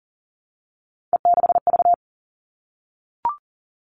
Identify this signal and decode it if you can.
Morse code, heard as a keyed tone, about 40 wpm, 700 Hz. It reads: E6V
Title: All Cut Numbers - 3-Digit 40wpm